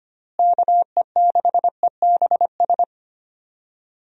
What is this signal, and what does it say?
Morse code, heard as a keyed tone, about 25 wpm, 700 Hz. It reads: KE6EBS
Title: USA Call Signs - 6 Characters - Repeated 3x - 25wpm